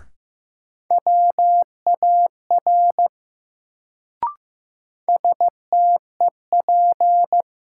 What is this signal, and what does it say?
Morse code, heard as a keyed tone, about 15 wpm, 700 Hz. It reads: WAR STEP
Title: Sentences from Top 400 Words - New Word Drill - 15wpm